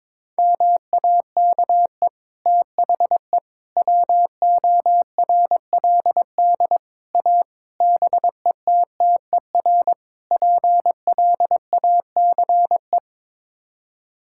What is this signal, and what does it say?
Morse code, heard as a keyed tone, about 22 wpm, 700 Hz. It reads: MAKE THE WORLD A BETTER PLACE
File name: practice / 22wpm / Sets of 6 Words - 22wpm